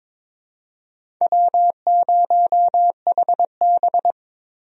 Morse code, heard as a keyed tone, about 22 wpm, 700 Hz. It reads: W0HB
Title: USA Call Signs - 4 Characters - 22wpm